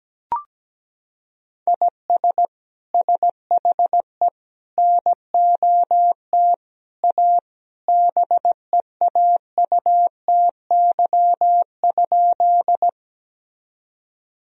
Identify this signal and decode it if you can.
Morse code, heard as a keyed tone, about 17 wpm, 700 Hz. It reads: IS SHE NOT A BEAUTY?
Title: Sentences from Top 1000 Words - Review Part 2 - 17wpm